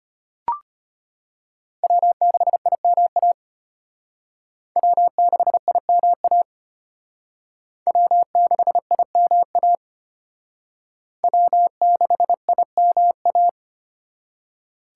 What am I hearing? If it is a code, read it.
Morse code, heard as a keyed tone, about 33 wpm, 700 Hz. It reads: W6IMA W6IMA W6IMA W6IMA